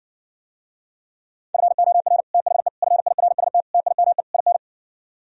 Morse code, heard as a keyed tone, about 59 wpm, 700 Hz. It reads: FOR THE FIRST TIME IN